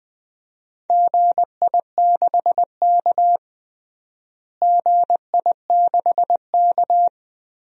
Morse code, heard as a keyed tone, about 20 wpm, 700 Hz. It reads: GI6K GI6K
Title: DXCC-100 Call Signs - 4 Characters - Repeated 3x - 20wpm